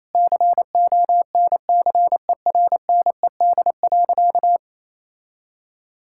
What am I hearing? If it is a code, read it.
Morse code, heard as a keyed tone, about 28 wpm, 700 Hz. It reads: CONCERNED.
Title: Sentences from Top 1000 Words - Review Part 6 - 28wpm